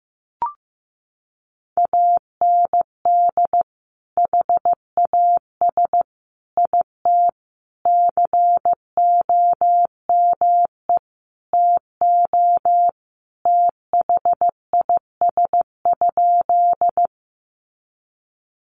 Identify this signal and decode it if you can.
Morse code, heard as a keyed tone, about 15 wpm, 700 Hz. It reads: AND HAS IT COME TO THIS?
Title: Sentences from Top 100 Words 15wpm